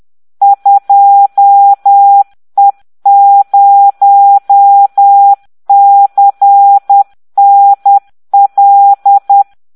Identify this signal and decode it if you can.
Morse code, heard as a keyed tone, about 10 wpm, 800 Hz. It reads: 2E0CNL